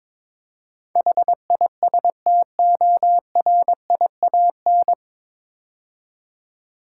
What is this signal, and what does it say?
Morse code, heard as a keyed tone, about 22 wpm, 700 Hz. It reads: HISTORIAN
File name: 9 Letter Words - First 3 Letters Given - 22wpm